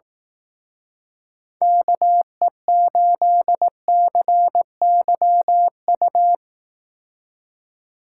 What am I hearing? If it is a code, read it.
Morse code, heard as a keyed tone, about 18 wpm, 700 Hz. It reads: KE8CYU
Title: Speed Racing - USA Call Signs 15wpm